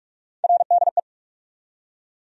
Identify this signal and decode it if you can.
Morse code, heard as a keyed tone, about 44 wpm, 700 Hz. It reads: RDE
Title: USA Call Sign Postfixes - 3 Letters 45wpm